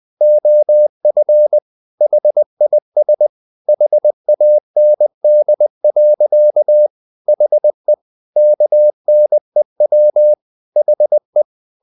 Morse code, heard as a keyed tone, about 20 wpm, 600 Hz. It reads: OF HIS HAND. HE KNEW HE
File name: the old man and the sea / the old man and the sea part 1 20wpm